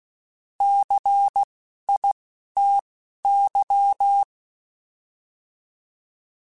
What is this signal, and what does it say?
Morse code, heard as a keyed tone, about 16 wpm, spaced as for 11 wpm, 800 Hz. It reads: CITY